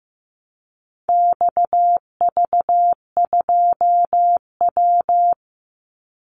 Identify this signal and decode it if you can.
Morse code, heard as a keyed tone, about 15 wpm, 700 Hz. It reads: XV2W
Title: Call Signs Repeated 3x - 15wpm